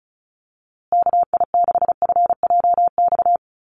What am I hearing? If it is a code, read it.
Morse code, heard as a keyed tone, about 34 wpm, 700 Hz. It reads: KI6FJX
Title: USA Call Signs Repeated 3x 35wpm